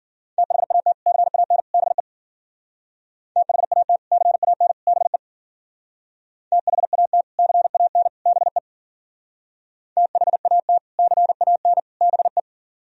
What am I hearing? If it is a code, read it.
Morse code, heard as a keyed tone, about 49 wpm, 700 Hz. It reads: THAT CAN BE THAT CAN BE THAT CAN BE THAT CAN BE
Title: Speed-Racing - Sets of 3 Words 40wpm